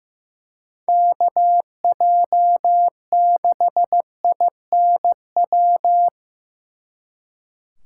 Morse code, heard as a keyed tone, about 18 wpm, 700 Hz. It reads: KJ6INW
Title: Speed Racing - USA 2x3 Call Signs 15wpm